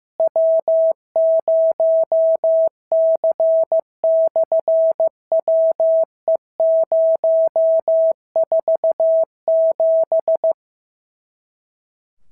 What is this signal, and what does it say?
Morse code, heard as a keyed tone, about 15 wpm, 650 Hz. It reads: W0C/WE047
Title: SOTA QSO Practice 15wpm